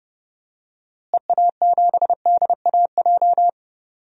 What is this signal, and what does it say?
Morse code, heard as a keyed tone, about 30 wpm, 700 Hz. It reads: EA7DAJ
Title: DXCC-100 Call Signs - 6 Characters - Repeated 2x - 30wpm